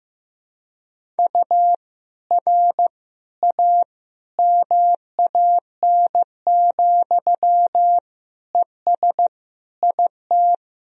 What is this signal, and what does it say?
Morse code, heard as a keyed tone, about 15 wpm, 700 Hz. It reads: U R A MAN, ES IT
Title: Sentences from Top 200 Words and Abbreviations 15wpm